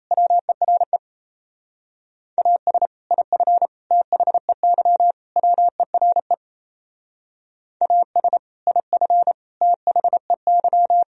Morse code, heard as a keyed tone, about 35 wpm, 700 Hz. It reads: WERE AS IF THEY WERE AS IF THEY